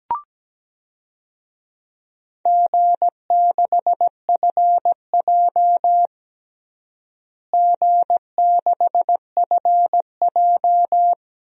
Morse code, heard as a keyed tone, about 17 wpm, 700 Hz. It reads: G6FJ G6FJ
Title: DXCC-100 Call Signs - 4 Characters - Repeated 3x - 17wpm